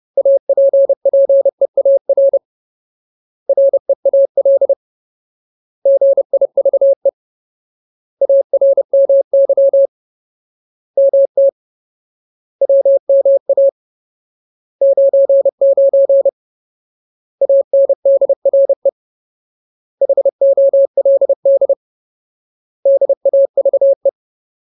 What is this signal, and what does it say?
Morse code, heard as a keyed tone, about 30 wpm, 550 Hz. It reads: APPEAR REAL GIVE ARMY MT WMA 99 ANDRE HOLD DAVE
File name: Ultimate Word List / 30WPM Ultimate